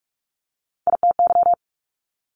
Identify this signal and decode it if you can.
Morse code, heard as a keyed tone, about 46 wpm, 700 Hz. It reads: ITY
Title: Multi-letter Phonemes Quick Fire 45wpm